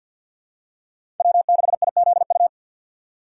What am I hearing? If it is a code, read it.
Morse code, heard as a keyed tone, about 52 wpm, 700 Hz. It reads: W6IZU
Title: USA Call Signs - 6x Character Spacing - 50wpm